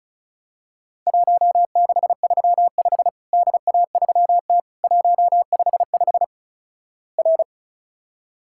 Morse code, heard as a keyed tone, about 36 wpm, 700 Hz. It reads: 1635 DA3T 155 R
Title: Worked All Europe DX Contest - QTC Exchange - 35wpm